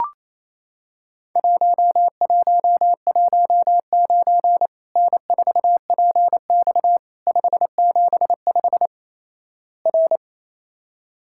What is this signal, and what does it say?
Morse code, heard as a keyed tone, about 28 wpm, 700 Hz. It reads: 1119 N4PX 575 R
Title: Worked All Europe DX Contest - QTC Exchange - 28wpm